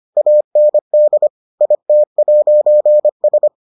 Morse code, heard as a keyed tone, about 25 wpm, 600 Hz. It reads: AND IT'S